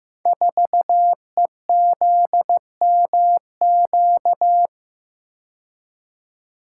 Morse code, heard as a keyed tone, about 15 wpm, 700 Hz. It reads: 4EZMQ
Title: Code Groups - 5 Letter:Numbers - 15wpm